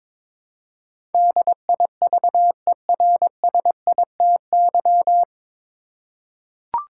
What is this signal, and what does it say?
Morse code, heard as a keyed tone, about 22 wpm, 700 Hz. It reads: DIVERSITY
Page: Morse Code Ninja practice 22wpm